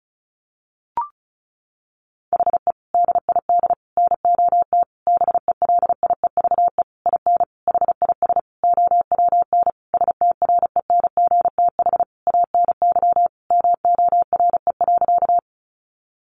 Morse code, heard as a keyed tone, about 34 wpm, 700 Hz. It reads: HE DID NOT BELIEVE IN HIS OWN STRENGTH ANY MORE.
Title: Sentences from Top 1300 Words 35wpm